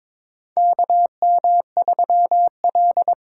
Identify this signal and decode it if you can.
Morse code, heard as a keyed tone, about 22 wpm, 700 Hz. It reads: KM3L
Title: USA Call Signs - 4 Characters - 4x Character Spacing - 22wpm